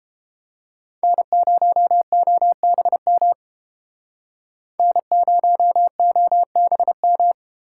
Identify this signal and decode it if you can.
Morse code, heard as a keyed tone, about 32 wpm, 700 Hz. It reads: N0OBM N0OBM